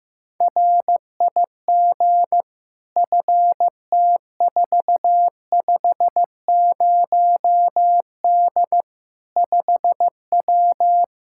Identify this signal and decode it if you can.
Morse code, heard as a keyed tone, about 15 wpm, 700 Hz. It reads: RIG FT450D 5W
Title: Practice QSO - Exchange 2 - Info - 15wpm